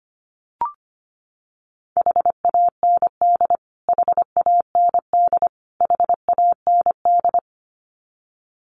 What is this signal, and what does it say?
Morse code, heard as a keyed tone, about 25 wpm, 700 Hz. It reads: HAND HAND HAND
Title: Top 500 Words Repeated 25wpm